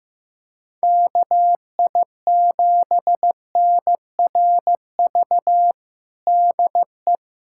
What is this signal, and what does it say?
Morse code, heard as a keyed tone, about 15 wpm, 700 Hz. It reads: KI7NRV DE
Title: USA Callsign DE Callsign 15wpm